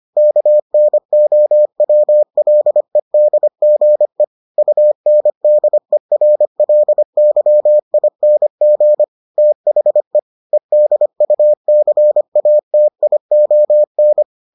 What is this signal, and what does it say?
Morse code, heard as a keyed tone, about 25 wpm, 600 Hz. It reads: KNOWLEDGE UNDERLYING THE EDUCATION